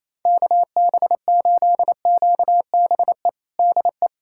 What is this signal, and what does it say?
Morse code, heard as a keyed tone, about 28 wpm, 700 Hz. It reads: KB8QBE DE
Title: USA Callsign DE Callsign 28wpm